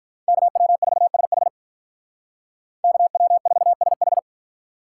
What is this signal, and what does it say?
Morse code, heard as a keyed tone, about 44 wpm, 700 Hz. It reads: KW4SH KW4SH